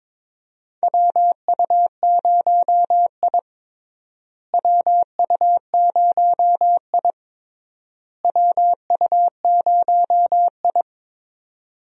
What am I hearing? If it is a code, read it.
Morse code, heard as a keyed tone, about 22 wpm, 700 Hz. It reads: WU0I WU0I WU0I
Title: USA Call Signs - 4 Characters - Repeated 3x - 22wpm